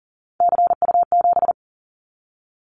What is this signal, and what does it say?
Morse code, heard as a keyed tone, about 40 wpm, 700 Hz. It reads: CU7
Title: Callsign Prefixes for Top 50 DXCC - 40wpm